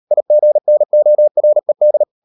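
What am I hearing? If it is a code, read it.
Morse code, heard as a keyed tone, about 38 wpm, 600 Hz. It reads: IGNORED
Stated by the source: Morse code practice at 38 WPM, sending Ken Follett's novel Never in English, from CW med Gnister.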